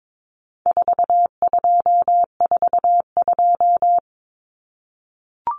Morse code, heard as a keyed tone, about 22 wpm, 700 Hz. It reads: 4242
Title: Numbers - 4-Digit 22wpm